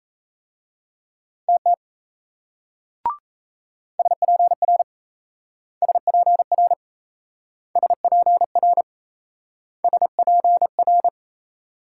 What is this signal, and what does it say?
Morse code, heard as a keyed tone, about 37 wpm, 700 Hz. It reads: TT SPR SPR SPR SPR